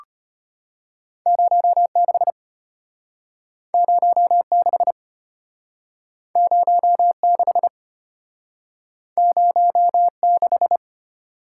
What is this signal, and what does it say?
Morse code, heard as a keyed tone, about 34 wpm, 700 Hz. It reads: 06 06 06 06